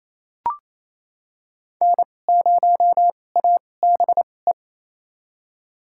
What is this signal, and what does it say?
Morse code, heard as a keyed tone, about 28 wpm, spaced as for 20 wpm, 700 Hz. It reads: N0ABE
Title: USA Call Signs - 5 Characters - 2x Character Spacing - 28wpm